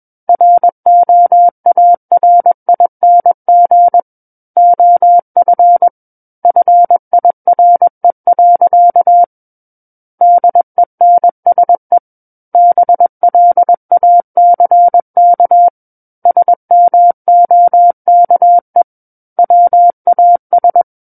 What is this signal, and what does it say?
Morse code, heard as a keyed tone, about 21 wpm, 700 Hz. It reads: ROARING OF FIRE. DENSE BLACK SMOKE WAS
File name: Book One / 12-Chapter 12 - 21 WPM